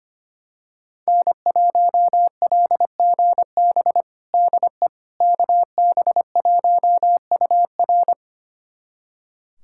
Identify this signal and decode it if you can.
Morse code, heard as a keyed tone, about 25 wpm, 700 Hz. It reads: N1LGB DE KB1UR
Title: USA Callsign DE Callsign - 5 Characters - 25wpm